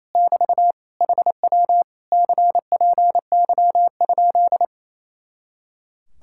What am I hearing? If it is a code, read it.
Morse code, heard as a keyed tone, about 26 wpm, 700 Hz. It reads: <BT> HW CPY?
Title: Speed-Racing Common QSO Phrases - 28wpm